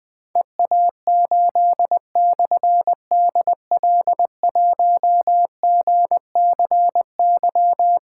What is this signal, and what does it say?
Morse code, heard as a keyed tone, about 20 wpm, 700 Hz. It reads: EA8/DL1GCY
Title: Difficult Call Signs 20wpm